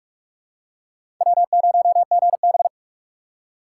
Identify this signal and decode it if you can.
Morse code, heard as a keyed tone, about 42 wpm, 700 Hz. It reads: W0GB
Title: USA Call Signs - 4 Characters - 6x Character Spacing - 45wpm